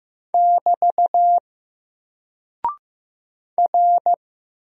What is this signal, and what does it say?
Morse code, heard as a keyed tone, about 15 wpm, 700 Hz. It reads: <BT> R
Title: Single Character (Letter:Number:Punctuation:Prosign) Alt 15wpm